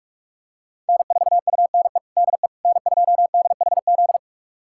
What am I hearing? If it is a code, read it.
Morse code, heard as a keyed tone, about 47 wpm, 700 Hz. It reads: N4UNE DE N2DHZ
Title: USA Callsign DE Callsign - 5 Characters - 45wpm